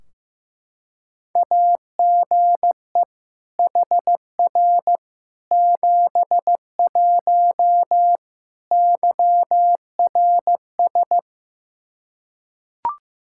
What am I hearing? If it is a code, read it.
Morse code, heard as a keyed tone, about 16 wpm, 700 Hz. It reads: AGE HR 71 YRS
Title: Practice QSO - Exchange 3 - More Info - 15wpm